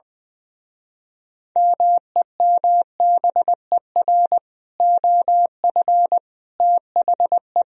Morse code, heard as a keyed tone, about 20 wpm, 700 Hz. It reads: MEMBER OF THE
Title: Speed-Racing - Sets of 3 Words 20wpm